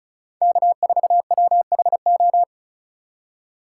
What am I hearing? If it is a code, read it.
Morse code, heard as a keyed tone, about 34 wpm, 700 Hz. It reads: K4WHO
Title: USA Call Signs - 4x Character Spacing - 35wpm